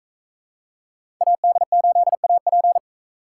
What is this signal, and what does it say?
Morse code, heard as a keyed tone, about 40 wpm, 700 Hz. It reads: AD8AP